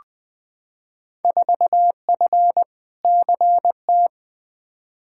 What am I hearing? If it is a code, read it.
Morse code, heard as a keyed tone, about 21 wpm, 700 Hz. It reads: 4F CT
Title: Field Day Exchanges Encore - 20wpm